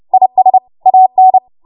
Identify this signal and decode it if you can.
Morse code, heard as a keyed tone, about 30 wpm, 750 Hz. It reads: IS AN
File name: Morse / 30WPM